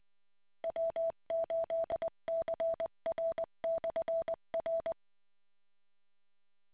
Morse code, heard as a keyed tone, about 20 wpm, 660 Hz. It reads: W8CR/R